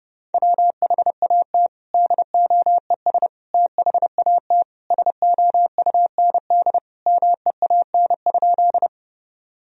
Morse code, heard as a keyed tone, about 30 wpm, 700 Hz. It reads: WHAT DOES THAT SOUND MEAN?
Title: Sentences from Top 1000 Words - Review Part 1 - 30wpm